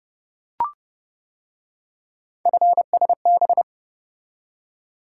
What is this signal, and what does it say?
Morse code, heard as a keyed tone, about 29 wpm, 700 Hz. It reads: FSB